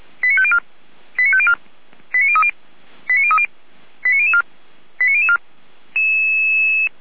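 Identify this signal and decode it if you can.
Five-tone selective call (ZVEI), heard as quick runs of five tones, tones of about 80 ms. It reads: 89504 89504 89030 89030 89004 89004
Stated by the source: ZVEI five-tone call received by SDR, from feuerwehr-orlamuende.org (Orlamünde fire brigade).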